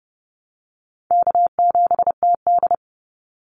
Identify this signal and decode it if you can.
Morse code, heard as a keyed tone, about 30 wpm, 700 Hz. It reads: K7TD
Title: Call Signs - 30wpm